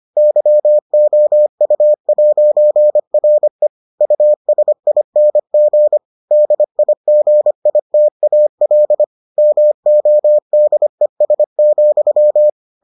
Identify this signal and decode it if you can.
Morse code, heard as a keyed tone, about 25 wpm, 600 Hz. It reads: YOU'RE USING DIGITAL MODES,